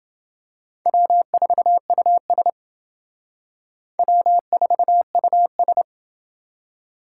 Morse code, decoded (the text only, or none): W4US W4US